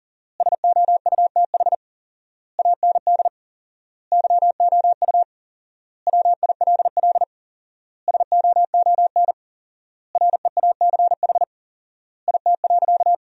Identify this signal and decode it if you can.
Morse code, heard as a keyed tone, about 37 wpm, 700 Hz. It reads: SOUTH AND YOU WILL SOON REACH IT.
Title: 4x Word-Spacing - Sentences from Top 400 Words - 40wpm